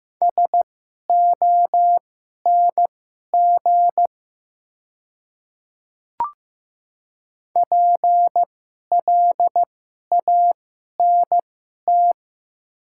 Morse code, heard as a keyed tone, about 15 wpm, spaced as for 11 wpm, 700 Hz. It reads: SONG PLANT
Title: 2x Character Spacing - Top 500 Words - 15wpm